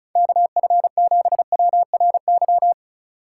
Morse code, heard as a keyed tone, about 34 wpm, 700 Hz. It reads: KF7WRY
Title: USA Call Signs - 3x Character Spacing - 35wpm